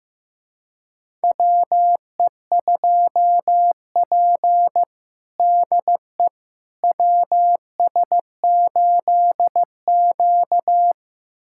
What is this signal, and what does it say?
Morse code, heard as a keyed tone, about 15 wpm, 700 Hz. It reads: WE2P DE WS8Q
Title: USA Callsign DE Callsign - 4 Characters - 15wpm